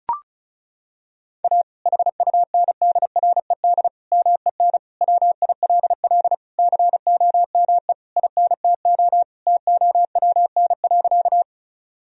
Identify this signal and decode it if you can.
Morse code, heard as a keyed tone, about 36 wpm, 700 Hz. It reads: A HUNDRED MEN WILL COME INTO TOWN.
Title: Sentences from Top 1000 Words - Review Part 2 - 35wpm